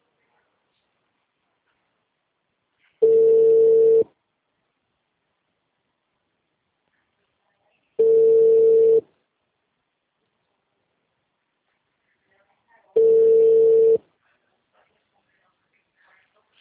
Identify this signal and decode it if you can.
Ringback tone (China, 450 Hz).